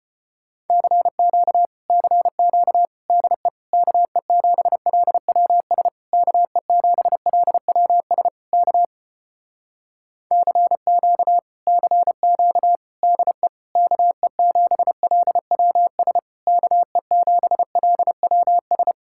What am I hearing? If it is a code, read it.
Morse code, heard as a keyed tone, about 33 wpm, 700 Hz. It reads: CQ CQ DE KE7LWS KE7LWS K CQ CQ DE KE7LWS KE7LWS